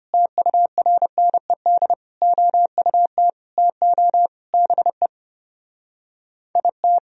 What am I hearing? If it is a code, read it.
Morse code, heard as a keyed tone, about 32 wpm, 700 Hz. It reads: TURNED OUT TO BE IT